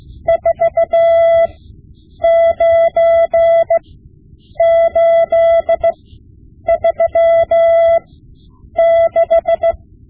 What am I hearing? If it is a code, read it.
Morse code, heard as a keyed tone, about 15 wpm, spaced as for 8 wpm, 670 Hz. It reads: 49836
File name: Morse Lessons / Morse Lessons / Tape-5B